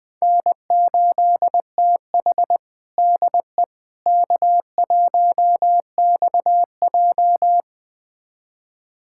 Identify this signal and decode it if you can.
Morse code, heard as a keyed tone, about 20 wpm, 700 Hz. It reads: N8TH DE K1XJ
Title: USA Callsign DE Callsign - 4 Characters - 20wpm